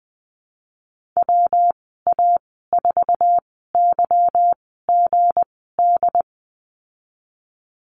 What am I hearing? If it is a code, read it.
Morse code, heard as a keyed tone, about 20 wpm, spaced as for 14 wpm, 700 Hz. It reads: WA4YGD
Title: 167 Lesson 27g - Number 7 - Call Signs 20wpm